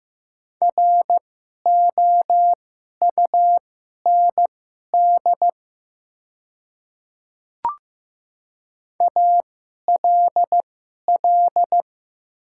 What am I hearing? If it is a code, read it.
Morse code, heard as a keyed tone, about 15 wpm, spaced as for 11 wpm, 700 Hz. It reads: ROUND ALL
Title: 2x Character Spacing - Top 500 Words - 15wpm